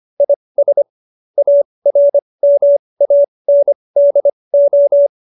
Morse code, heard as a keyed tone, about 25 wpm, spaced as for 20 wpm, 570 Hz. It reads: IS ARMANDO
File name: qso215b 20